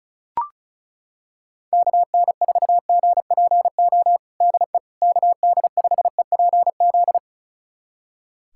Morse code, heard as a keyed tone, about 34 wpm, 700 Hz. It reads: KN4GPO DE KD5EPZ
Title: USA Callsign DE Callsign - 6 Characters - 35wpm